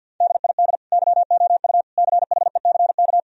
Morse code, heard as a keyed tone, about 50 wpm, 700 Hz. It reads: DID YOU CHECK
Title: Sentences from Top 1000 Words - Review Part 4 - 50wpm